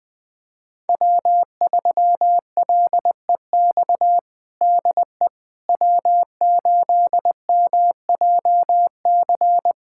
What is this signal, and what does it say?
Morse code, heard as a keyed tone, about 20 wpm, 700 Hz. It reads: W3LEX DE W8MJC
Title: USA Callsign DE Callsign - 5 Characters - 20wpm